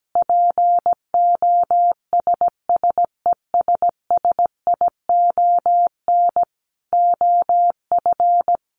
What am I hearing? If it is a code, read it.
Morse code, heard as a keyed tone, about 17 wpm, 700 Hz. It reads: POSSESSION OF
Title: Sentences from Top 1400 Words 17wpm